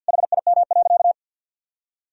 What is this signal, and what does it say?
Morse code, heard as a keyed tone, about 51 wpm, 700 Hz. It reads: HIM.
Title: Sentences from Top 400 Words and Abbreviations 50wpm